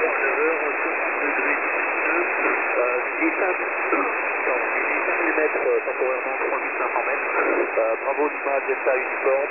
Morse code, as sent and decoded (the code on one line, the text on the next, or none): none